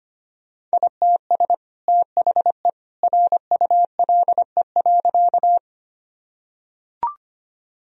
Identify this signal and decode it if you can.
Morse code, heard as a keyed tone, about 25 wpm, 700 Hz. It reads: ITS THE RULE.
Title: Sentences from Top 1000 Words - Review Part 2 - 25wpm